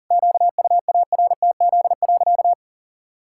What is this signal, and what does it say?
Morse code, heard as a keyed tone, about 40 wpm, 700 Hz. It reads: QUARTZ.
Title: Pangram Challenge v2 - 40wpm